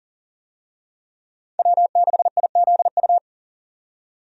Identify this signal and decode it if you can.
Morse code, heard as a keyed tone, about 40 wpm, 700 Hz. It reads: W6IZU
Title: USA Call Signs - 5 Characters - 40wpm